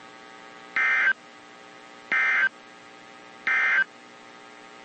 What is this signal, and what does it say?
Emergency Alert System (SAME) tones, heard as harsh buzzing bursts. It is heard from 0.6 to 3.9 s.
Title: WZ2523 - NOAA Weather Radio